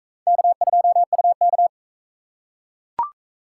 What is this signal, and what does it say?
Morse code, heard as a keyed tone, about 40 wpm, 700 Hz. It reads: K2UK